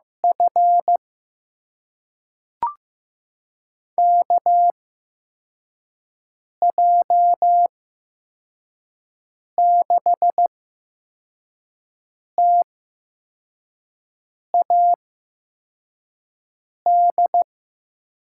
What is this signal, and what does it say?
Morse code, heard as a keyed tone, about 15 wpm, spaced as for 4 wpm, 700 Hz. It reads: F KJ6TAD
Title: USA Call Signs - 8x Character Spacing - 15wpm